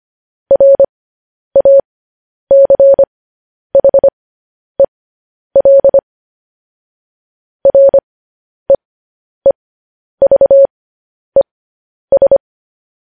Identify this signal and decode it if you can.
Morse code, heard as a keyed tone, about 25 wpm, spaced as for 10 wpm, 550 Hz. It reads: RACHEL REEVES
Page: news-headlines-in-morse-code-at-10-wpm